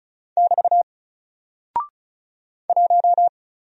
Morse code, heard as a keyed tone, about 36 wpm, 700 Hz. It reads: <BT> 1
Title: Single Character (Letter:Number:Punctuation:Prosign) Alt 35wpm